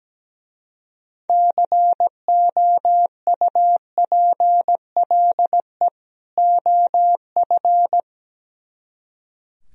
Morse code, heard as a keyed tone, about 17 wpm, 700 Hz. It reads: COUPLE OF
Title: Speed-Racing - Sets of 2 Words 17wpm